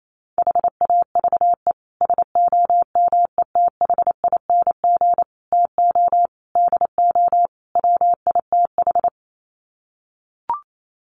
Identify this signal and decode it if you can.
Morse code, heard as a keyed tone, about 28 wpm, 700 Hz. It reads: HAVE SOMETHING TO DO WITH
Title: Sets of 5 Words Encore 28wpm